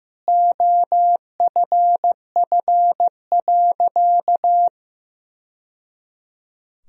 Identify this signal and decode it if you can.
Morse code, heard as a keyed tone, about 15 wpm, 700 Hz. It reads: OFF.
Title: Sentences from Top 1000 Words - Review Part 1 - 15wpm